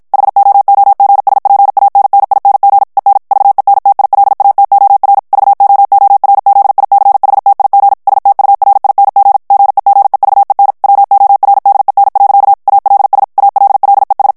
Morse code, heard as a keyed tone, about 50 wpm, 800 Hz. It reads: HOOGSPANNING EN VENTILATOR VOORZICHTIG STURING GEGEVEN WORDEN. ALS ALLES
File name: HOLLANDAIS texte clair 2 vitesse 50 mots minute